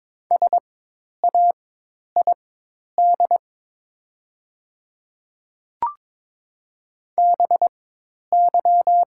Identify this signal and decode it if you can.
Morse code, heard as a keyed tone, about 21 wpm, spaced as for 10 wpm, 700 Hz. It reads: SAID BY
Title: 4x Character Spacing - Top 100 Words - 22wpm